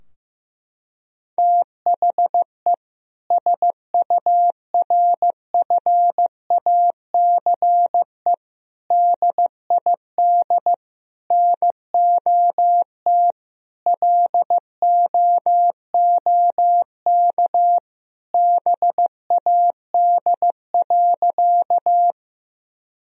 Morse code, heard as a keyed tone, about 15 wpm, 700 Hz. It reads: THE SURFACE DID NOT LOOK BAD.
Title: Sentences from Top 1000 Words - Review Part 3 - 15wpm